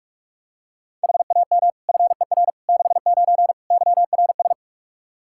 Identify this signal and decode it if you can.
Morse code, heard as a keyed tone, about 45 wpm, 700 Hz. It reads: HAM FER 69 YRS
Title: K1USN SST QSO Practice 45wpm